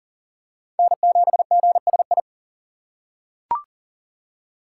Morse code, heard as a keyed tone, about 40 wpm, 700 Hz. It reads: N7GSI
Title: USA Call Signs - 2x Character Spacing - 40wpm